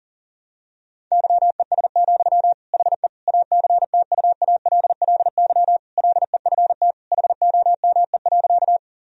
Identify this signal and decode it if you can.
Morse code, heard as a keyed tone, about 40 wpm, 700 Hz. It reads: YES, HE ACTUALLY LEFT HOME.
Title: Sentences from Top 1000 Words - Review Part 4 - 40wpm